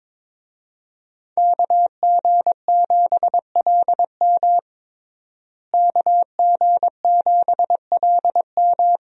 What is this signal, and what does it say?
Morse code, heard as a keyed tone, about 22 wpm, 700 Hz. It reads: KG7LM KG7LM
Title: USA Call Signs - 5 Characters - Repeated 2x - 22wpm